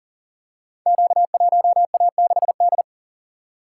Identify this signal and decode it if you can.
Morse code, heard as a keyed tone, about 38 wpm, 700 Hz. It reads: Q1A6D
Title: Code Groups - 5 Letter:Numbers - 40wpm